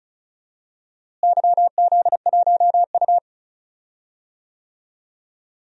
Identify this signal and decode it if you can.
Morse code, heard as a keyed tone, about 36 wpm, 700 Hz. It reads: YZ1U
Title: DXCC-100 Call Signs - 4 Characters - Repeated 3x - 35wpm